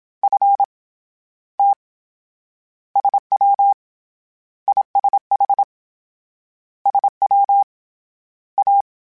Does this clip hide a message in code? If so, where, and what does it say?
Yes: Morse code from 0.2 to 8.8 s, heard as a keyed tone, about 27 wpm, 800 Hz. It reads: F T SW ISH SW A